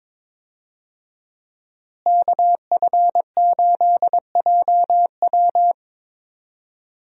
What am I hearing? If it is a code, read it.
Morse code, heard as a keyed tone, about 22 wpm, 700 Hz. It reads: KF8JW
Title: USA Call Signs - 5 Characters - Repeated 2x - 22wpm